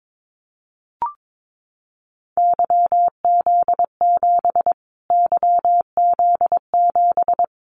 Morse code, heard as a keyed tone, about 22 wpm, 700 Hz. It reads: YZ7 YZ7
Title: Callsign Prefixes for Top 25 DXCC - Repeated 3x - 22wpm